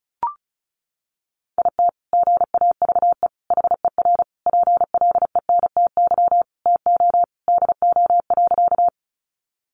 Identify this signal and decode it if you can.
Morse code, heard as a keyed tone, about 34 wpm, 700 Hz. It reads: IT GAVE HER PLENTY TO DO.
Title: Sentences from Top 1300 Words 35wpm